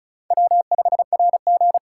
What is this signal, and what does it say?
Morse code, heard as a keyed tone, about 36 wpm, 700 Hz. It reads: W5RG
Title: USA Call Signs - 4x Character Spacing - 35wpm